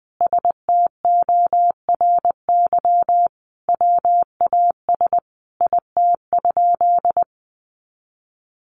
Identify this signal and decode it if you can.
Morse code, heard as a keyed tone, about 20 wpm, 700 Hz. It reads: STORY WAS IT?
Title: Sentences from Top 300 Words 20wpm